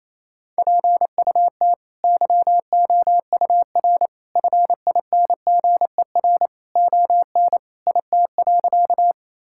Morse code, heard as a keyed tone, about 28 wpm, 700 Hz. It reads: PUT YOUR FINGER ON IT.
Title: Sentences from Top 1600 Words 28wpm